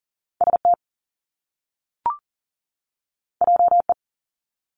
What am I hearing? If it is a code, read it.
Morse code, heard as a keyed tone, about 40 wpm, 700 Hz. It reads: ST JE
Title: Top Two Letter Combinations in Words 40wpm